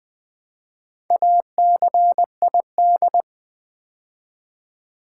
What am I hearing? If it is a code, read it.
Morse code, heard as a keyed tone, about 20 wpm, 700 Hz. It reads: ACID
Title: Build A Word - 2 Letters At A Time - 4-Letter Words - 20wpm